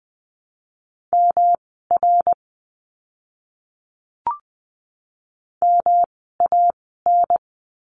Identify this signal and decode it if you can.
Morse code, heard as a keyed tone, about 20 wpm, spaced as for 15 wpm, 700 Hz. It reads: MR MAN
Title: 244 Lesson 39f - Letter z - Words 20wpm